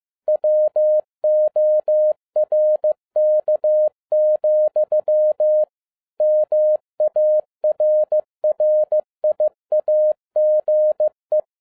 Morse code, heard as a keyed tone, about 15 wpm, 610 Hz. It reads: WORK, MARRIAGE